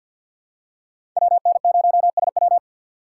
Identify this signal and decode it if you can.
Morse code, heard as a keyed tone, about 47 wpm, 700 Hz. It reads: WN0SW
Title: USA Call Signs - 5 Characters - 3x Character Spacing - 50wpm